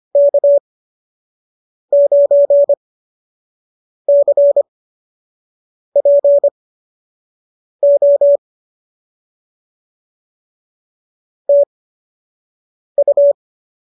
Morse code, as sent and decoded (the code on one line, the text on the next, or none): -.- ----. -.-. .--. --- / - ..-
K9CPO TU